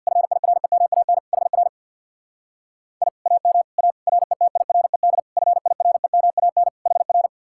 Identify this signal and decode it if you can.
Morse code, heard as a keyed tone, about 64 wpm, 700 Hz. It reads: FIREMAN HR I AM A RETIRED FIREMAN HR